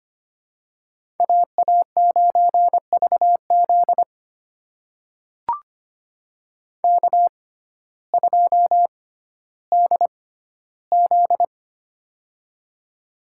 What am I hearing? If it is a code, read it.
Morse code, heard as a keyed tone, about 25 wpm, spaced as for 14 wpm, 700 Hz. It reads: AA9VZ K2DZ
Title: USA Call Signs - 6x Character Spacing - 25wpm